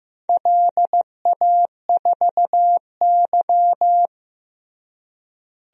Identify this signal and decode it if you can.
Morse code, heard as a keyed tone, about 15 wpm, 700 Hz. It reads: LA4Y
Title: DXCC-100 Call Signs - 4 Characters - Repeated 3x - 15wpm